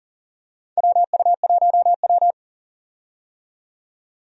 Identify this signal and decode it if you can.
Morse code, heard as a keyed tone, about 40 wpm, 700 Hz. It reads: WU1W
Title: USA Call Signs - 4 Characters - 40wpm